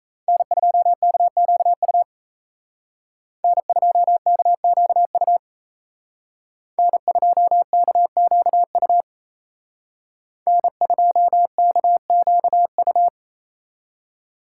Morse code, heard as a keyed tone, about 33 wpm, 700 Hz. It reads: N2KQU N2KQU N2KQU N2KQU
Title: Speed Racing - USA 1x3 Call Signs 28wpm